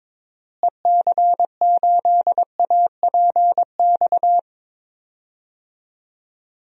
Morse code, heard as a keyed tone, about 22 wpm, 700 Hz. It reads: EC8APX
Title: DXCC-100 Call Signs - 6 Characters - Repeated 3x - 22wpm